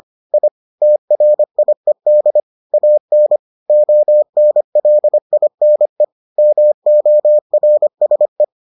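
Morse code, heard as a keyed tone, about 25 wpm, 600 Hz. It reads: I TRIED AN ONLINE MORSE
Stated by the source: Morse edition of the Foundations of Amateur Radio episodes.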